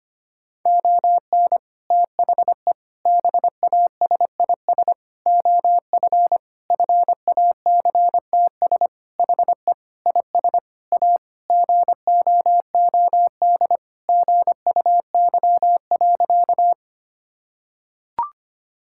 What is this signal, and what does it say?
Morse code, heard as a keyed tone, about 25 wpm, 700 Hz. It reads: ON THE BASIS OF FACTS HE IS A GOOD GUY.
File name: practice / 25wpm / Sentences with 3-Word Phrases - 25wpm